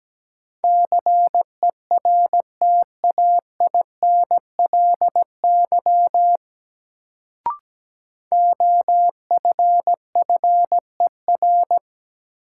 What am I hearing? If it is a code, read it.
Morse code, heard as a keyed tone, about 17 wpm, 700 Hz. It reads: CERTAINLY OFFER
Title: Sentences from Top 900 Words - New Word Drill - 17wpm